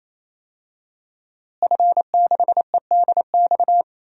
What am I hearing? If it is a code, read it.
Morse code, heard as a keyed tone, about 27 wpm, 700 Hz. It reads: F6EDX
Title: DXCC-100 Call Signs - 5 Characters - Repeated 3x - 28wpm